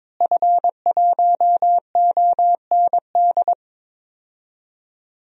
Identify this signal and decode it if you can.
Morse code, heard as a keyed tone, about 22 wpm, 700 Hz. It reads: F1OND
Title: DXCC-100 Call Signs - 5 Characters - Repeated 2x - 22wpm